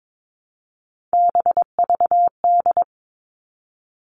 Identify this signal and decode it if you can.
Morse code, heard as a keyed tone, about 22 wpm, 700 Hz. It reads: BVD
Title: Three Challenging Letter-Number Combinations 22wpm